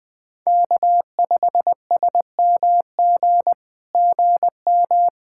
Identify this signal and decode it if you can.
Morse code, heard as a keyed tone, about 20 wpm, 700 Hz. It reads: K5SMG GM
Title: SOTA QSO Practice 20wpm